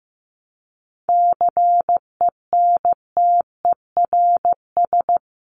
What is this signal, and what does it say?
Morse code, heard as a keyed tone, about 15 wpm, 700 Hz. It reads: CENTERS
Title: Sets of 5 Words Encore2 15wpm